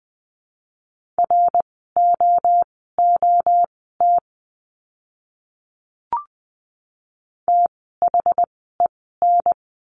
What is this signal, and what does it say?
Morse code, heard as a keyed tone, about 20 wpm, spaced as for 15 wpm, 700 Hz. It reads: ROOT THEN
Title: 068 Lesson 12f - Letter d - Words 20wpm